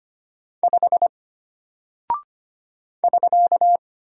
Morse code, heard as a keyed tone, about 25 wpm, 700 Hz. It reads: 5 <SK>
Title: Single Character (Letter:Number:Punctuation:Prosign) Alt 25wpm